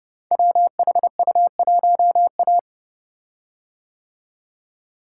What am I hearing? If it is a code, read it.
Morse code, heard as a keyed tone, about 30 wpm, 700 Hz. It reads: WHU1A
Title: Code Groups - 5 Letter:Numbers - 30wpm